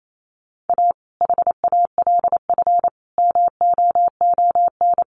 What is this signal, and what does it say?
Morse code, heard as a keyed tone, about 28 wpm, 700 Hz. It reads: A HALF MOON
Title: Sentences from Top 1100 Words 28wpm